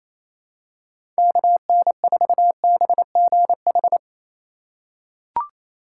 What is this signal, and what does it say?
Morse code, heard as a keyed tone, about 27 wpm, 700 Hz. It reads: KN4BGH
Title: USA Call Signs - 6 Characters - 4x Character Spacing - 28wpm